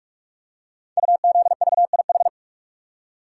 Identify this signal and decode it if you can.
Morse code, heard as a keyed tone, about 44 wpm, 700 Hz. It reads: UZVIH
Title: Code Groups - 5 Letters - 45wpm